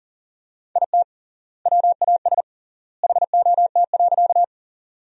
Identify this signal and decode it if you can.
Morse code, heard as a keyed tone, about 40 wpm, 700 Hz. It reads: IT WAS HOT.
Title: 3x Word-Spacing - Sentences from Top 500 Words - 40wpm